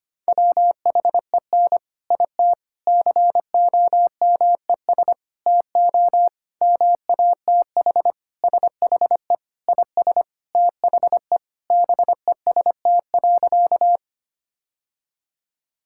Morse code, heard as a keyed tone, about 25 wpm, 700 Hz. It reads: WHEN IT COMES TO MATH SHE IS THE BEST.
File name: practice / 25wpm / Sentences with 3-Word Phrases - 25wpm